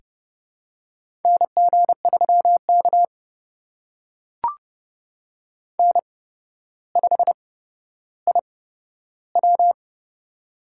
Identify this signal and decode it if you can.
Morse code, heard as a keyed tone, about 30 wpm, spaced as for 13 wpm, 700 Hz. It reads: NG3K N5IW
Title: USA Call Signs - 4 Characters - 8x Character Spacing - 30wpm